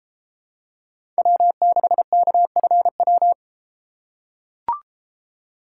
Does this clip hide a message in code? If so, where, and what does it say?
Yes: Morse code from 1.2 to 3.3 s, heard as a keyed tone, about 32 wpm, 700 Hz. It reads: W6KFW